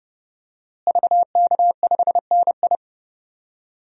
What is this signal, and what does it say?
Morse code, heard as a keyed tone, about 30 wpm, 700 Hz. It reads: VK5NI
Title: DXCC-100 Call Signs - 5 Characters - Repeated 2x - 30wpm